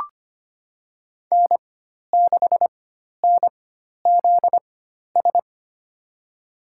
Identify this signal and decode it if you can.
Morse code, heard as a keyed tone, about 25 wpm, spaced as for 12 wpm, 700 Hz. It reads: N6NZS
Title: USA Call Signs - 5 Characters - 4x Character Spacing - 25wpm